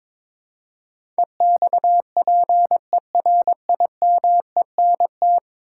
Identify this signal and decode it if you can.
Morse code, heard as a keyed tone, about 22 wpm, 700 Hz. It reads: EXPERIMENT
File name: practice / 22wpm / 10 Letter Words - First 3 Letters Given - 22wpm